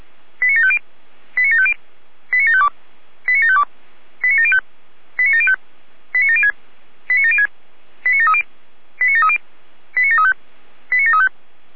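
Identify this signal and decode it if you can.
Five-tone selective call (ZVEI), heard as quick runs of five tones, tones of about 80 ms. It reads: 89750 89750 89752 89752 89705 89705 89706 89706 89730 89730 89735 89735
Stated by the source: ZVEI five-tone selective call received on an SDR.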